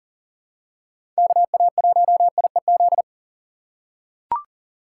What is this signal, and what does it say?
Morse code, heard as a keyed tone, about 40 wpm, 700 Hz. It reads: KA1IEZ
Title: USA Call Signs - 6 Characters - 2x Character Spacing - 40wpm